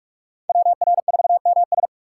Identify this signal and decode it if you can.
Morse code, heard as a keyed tone, about 45 wpm, 700 Hz. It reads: WR4MS
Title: USA Call Signs - 5 Characters - 2x Character Spacing - 45wpm